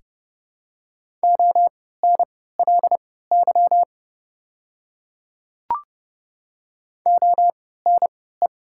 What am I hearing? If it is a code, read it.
Morse code, heard as a keyed tone, about 30 wpm, spaced as for 17 wpm, 690 Hz. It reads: ONLY ONE
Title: 3x Character Spacing - Top 200 Words - 30wpm